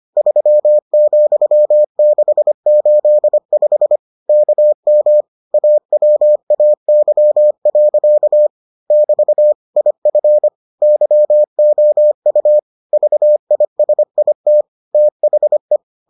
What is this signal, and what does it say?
Morse code, heard as a keyed tone, about 25 wpm, 600 Hz. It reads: 3,685 KM AWAY. <BT> IF YOU VISIT THE